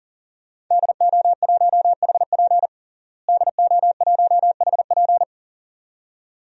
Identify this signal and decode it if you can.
Morse code, heard as a keyed tone, about 40 wpm, 700 Hz. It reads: DO1HP DO1HP
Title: DXCC-100 Call Signs - 5 Characters - Repeated 2x - 40wpm